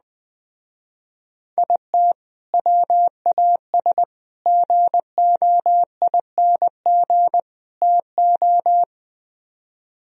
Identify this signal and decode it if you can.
Morse code, heard as a keyed tone, about 20 wpm, 700 Hz. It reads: IT WAS GOING TO
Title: Speed-Racing - Sets of 4 Words 17wpm